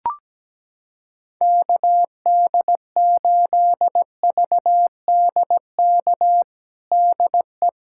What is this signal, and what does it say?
Morse code, heard as a keyed tone, about 17 wpm, 700 Hz. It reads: KD8VDK DE
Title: USA Callsign DE Callsign 17wpm